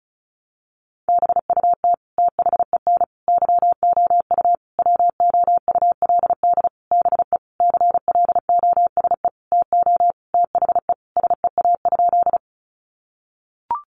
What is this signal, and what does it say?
Morse code, heard as a keyed tone, about 34 wpm, 700 Hz. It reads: BUT THEN YOU WOULD BE CLOSE TO THE SEA?
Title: Sentences from Top 300 Words 35wpm